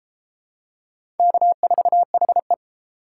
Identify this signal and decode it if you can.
Morse code, heard as a keyed tone, about 29 wpm, 700 Hz. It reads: K4HE